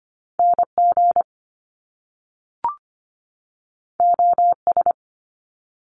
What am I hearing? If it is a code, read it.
Morse code, heard as a keyed tone, about 25 wpm, 700 Hz. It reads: NG OS
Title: Top Two Letter Combinations in Words 25wpm